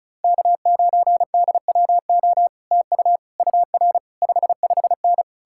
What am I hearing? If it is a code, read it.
Morse code, heard as a keyed tone, about 48 wpm, spaced as for 41 wpm, 700 Hz. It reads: K9DWO TU UR 55N
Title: Speed-Racing Common QSO Phrases - 35wpm